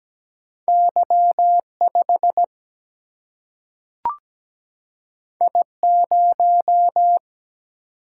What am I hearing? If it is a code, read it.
Morse code, heard as a keyed tone, about 17 wpm, 700 Hz. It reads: Y 5 I 0
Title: Single-Letter Single-Number 17wpm